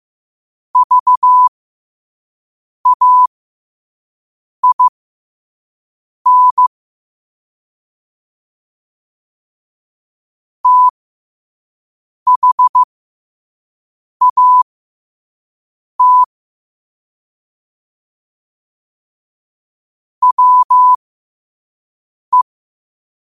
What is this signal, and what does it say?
Morse code, heard as a keyed tone, about 15 wpm, spaced as for 5 wpm, 1000 Hz. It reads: VAIN THAT WE